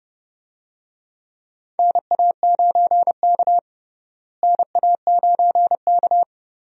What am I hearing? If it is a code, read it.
Morse code, heard as a keyed tone, about 31 wpm, 700 Hz. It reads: NA9K NA9K
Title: USA Call Signs - 4 Characters - Repeated 2x - 30wpm